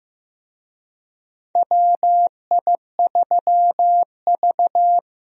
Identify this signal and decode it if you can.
Morse code, heard as a keyed tone, about 15 wpm, 700 Hz. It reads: WI3V
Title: USA Call Signs - 4 Characters - Repeated 2x - 15wpm